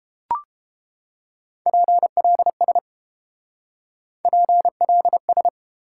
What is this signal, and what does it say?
Morse code, heard as a keyed tone, about 30 wpm, 700 Hz. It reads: PLS PLS